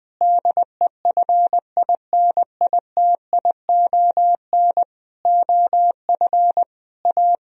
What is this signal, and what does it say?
Morse code, heard as a keyed tone, about 20 wpm, 700 Hz. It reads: DEFINITION OF A
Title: Sentences from Top 1000 Words - Review Part 4 - 20wpm